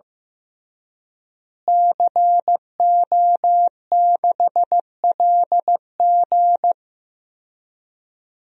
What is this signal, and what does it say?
Morse code, heard as a keyed tone, about 15 wpm, 700 Hz. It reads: CO6LG
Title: DXCC-100 Call Signs - 5 Characters - Repeated 3x - 15wpm